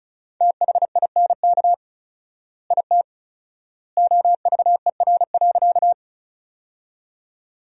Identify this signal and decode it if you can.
Morse code, heard as a keyed tone, about 34 wpm, 700 Hz. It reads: THINK IT OVER.
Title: 4x Word-Spacing - Sentences from Top 200 Words - 35wpm